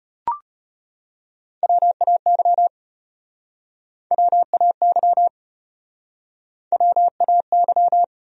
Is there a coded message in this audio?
Yes: Morse code from 1.6 to 8.0 s, heard as a keyed tone, about 36 wpm, 700 Hz. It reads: WAY WAY WAY